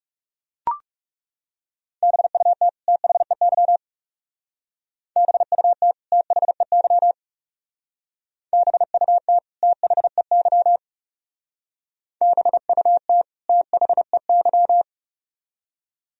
Morse code, heard as a keyed tone, about 40 wpm, 700 Hz. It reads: BUT THEY BUT THEY BUT THEY BUT THEY